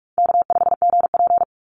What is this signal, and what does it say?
Morse code, heard as a keyed tone, about 44 wpm, 700 Hz. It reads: K5GP
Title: Call Signs Repeated 2x - 45wpm